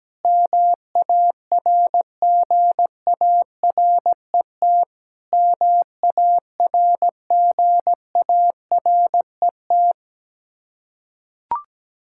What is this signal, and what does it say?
Morse code, heard as a keyed tone, about 17 wpm, 700 Hz. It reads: MARGARET MARGARET
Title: Practice QSO - Exchange 1 - Beginning - 17wpm